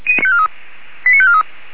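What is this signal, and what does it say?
Five-tone selective call (ZVEI), heard as quick runs of five tones, tones of about 70 ms. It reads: 89543
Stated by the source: ZVEI five-tone selective call received on SDR.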